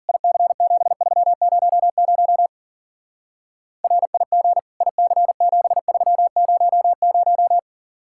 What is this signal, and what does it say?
Morse code, heard as a keyed tone, about 47 wpm, 700 Hz. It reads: IC7300 RIG IC7300
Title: Speed-Racing Common QSO Phrases Encore - 40wpm